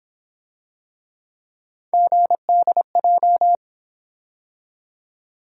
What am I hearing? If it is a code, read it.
Morse code, heard as a keyed tone, about 28 wpm, 700 Hz. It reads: GDJ